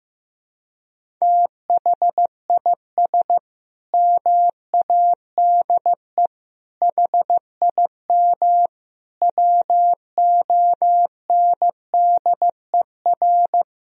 Morse code, heard as a keyed tone, about 16 wpm, 700 Hz. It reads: THIS MADE HIM WONDER
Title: Sentences from Top 500 Words and Abbreviations 15wpm